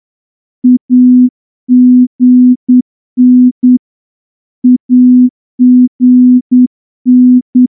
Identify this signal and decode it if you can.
Morse code, heard as a keyed tone, about 9 wpm, 250 Hz. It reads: AGN AGN
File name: morse-training-10WPM